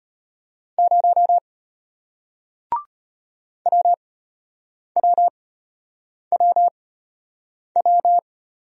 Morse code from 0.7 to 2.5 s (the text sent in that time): 0